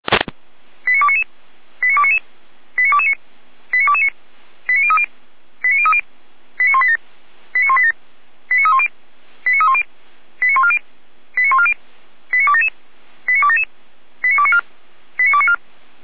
Five-tone selective call (ZVEI), heard as quick runs of five tones, tones of about 70 ms. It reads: 89299 89299 89229 89229 89030 89030 89187 89187 89310 89310 89140 89140 89277 89277 89294 89294
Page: /455/SDR/ZVEI/89165/